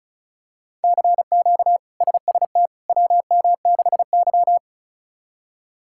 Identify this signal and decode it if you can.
Morse code, heard as a keyed tone, about 36 wpm, 700 Hz. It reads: CQ SST WM6Y
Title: K1USN SST QSO Practice - 35wpm